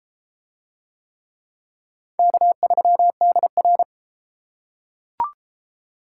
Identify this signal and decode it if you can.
Morse code, heard as a keyed tone, about 31 wpm, 700 Hz. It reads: K3DR